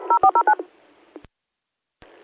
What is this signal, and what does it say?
DTMF, heard as telephone touch tones, tones of about 80 ms. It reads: 0409